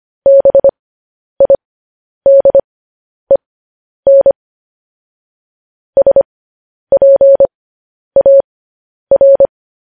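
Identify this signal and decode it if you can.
Morse code, heard as a keyed tone, about 25 wpm, spaced as for 10 wpm, 550 Hz. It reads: BIDEN SPAR
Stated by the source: Morse code transmission of news headlines.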